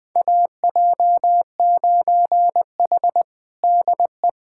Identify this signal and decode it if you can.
Morse code, heard as a keyed tone, about 20 wpm, 700 Hz. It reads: AJ9H DE
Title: USA Callsign DE Callsign - 4 Characters - 20wpm